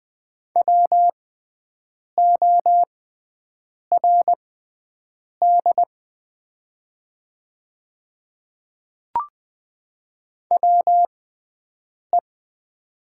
Morse code, heard as a keyed tone, about 20 wpm, spaced as for 7 wpm, 700 Hz. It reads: WORD WE
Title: 6x Character Spacing - Top 300 Words - 20wpm